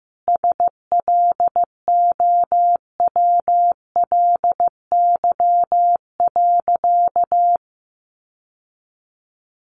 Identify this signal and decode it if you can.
Morse code, heard as a keyed tone, about 15 wpm, 700 Hz. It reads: SLOWLY.
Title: Sentences from Top 1100 Words 15wpm